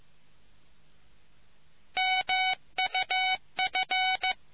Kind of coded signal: Morse code